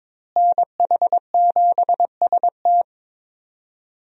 Morse code, heard as a keyed tone, about 22 wpm, 700 Hz. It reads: NH7ST
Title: USA Call Signs - 5 Characters - Repeated 3x - 22wpm